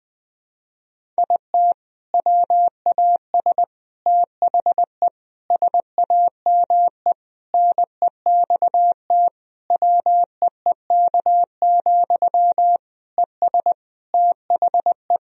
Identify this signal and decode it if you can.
Morse code, heard as a keyed tone, about 20 wpm, 700 Hz. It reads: IT WAS THE SAME NEXT WEEK, ES THE
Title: Sentences from Top 500 Words and Abbreviations 20wpm